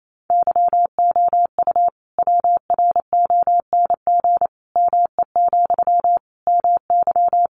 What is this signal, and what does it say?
Morse code, heard as a keyed tone, about 28 wpm, 700 Hz. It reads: YOU WRONG ME, MY
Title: Sentences from Top 900 Words 28wpm